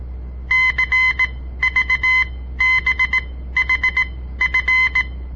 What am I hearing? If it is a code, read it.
Morse code, heard as a keyed tone, about 18 wpm, spaced as for 13 wpm, 2050 Hz. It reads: CVBHF
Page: On the air Morse code practice from the Southern Montana Amatuer Radio Association; all code, no read back